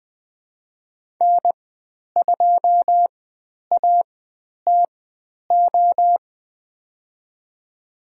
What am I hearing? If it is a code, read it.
Morse code, heard as a keyed tone, about 20 wpm, spaced as for 10 wpm, 700 Hz. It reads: N2ATO
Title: 102 Lesson 17g - Letter u - Call Signs 20wpm